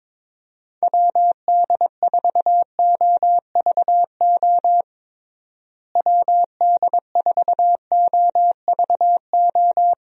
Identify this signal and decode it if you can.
Morse code, heard as a keyed tone, about 22 wpm, 700 Hz. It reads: WD4OVO WD4OVO
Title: USA Call Signs - 6 Characters - Repeated 2x - 22wpm